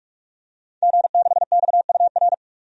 Morse code, heard as a keyed tone, about 47 wpm, 700 Hz. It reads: G6XUR
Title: DXCC-100 Call Signs - 5 Characters - 45wpm